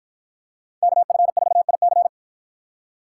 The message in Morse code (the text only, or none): KF4IC